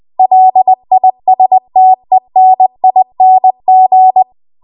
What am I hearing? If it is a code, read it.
Morse code, heard as a keyed tone, about 20 wpm, 750 Hz. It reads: LISTENING
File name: Archive / 20 WPM / 20WPM